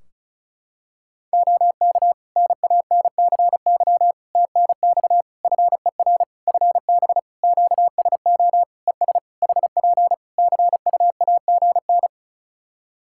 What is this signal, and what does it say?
Morse code, heard as a keyed tone, about 36 wpm, 700 Hz. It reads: OK NANCY TNX FER FB QSO ES HP CUAGN
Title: Practice QSO - Exchange 4 - Ending - 35wpm